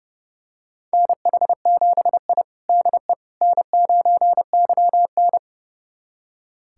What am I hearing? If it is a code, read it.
Morse code, heard as a keyed tone, about 32 wpm, 700 Hz. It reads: NH7I DE N9YN
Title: USA Callsign DE Callsign - 4 Characters - 30wpm